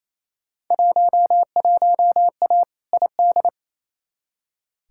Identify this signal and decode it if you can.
Morse code, heard as a keyed tone, about 28 wpm, 700 Hz. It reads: 11A ID
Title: Field Day Exchanges Encore - 28wpm